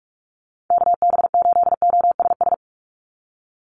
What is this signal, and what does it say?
Morse code, heard as a keyed tone, about 46 wpm, 690 Hz. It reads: KB8OSS
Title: USA Call Signs 45wpm